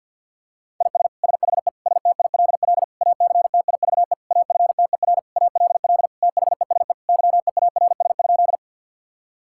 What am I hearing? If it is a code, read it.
Morse code, heard as a keyed tone, about 51 wpm, 700 Hz. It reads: IS SHE STILL ACTIVE AFTER ALL THESE YEARS?
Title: Sentences from Top 1000 Words - Review Part 6 - 50wpm